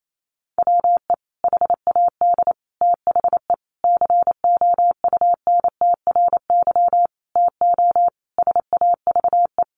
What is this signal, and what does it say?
Morse code, heard as a keyed tone, about 28 wpm, 700 Hz. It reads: WE HAD THE COUNTRY TO SAVE
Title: Sentences from Top 1100 Words 28wpm